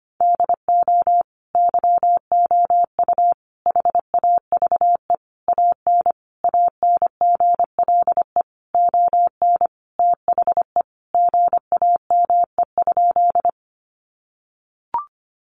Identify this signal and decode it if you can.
Morse code, heard as a keyed tone, about 25 wpm, 700 Hz. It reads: DO YOU HAVE AN ANGLE ON THE GAME?
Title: Sentences from Top 1500 Words 25wpm